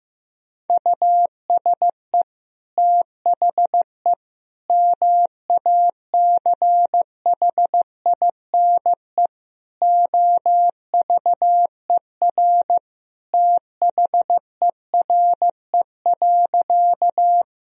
Morse code, heard as a keyed tone, about 16 wpm, 700 Hz. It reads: USE THE MACHINE OVER THERE.
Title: Sentences from Top 1000 Words - Review Part 2 - 15wpm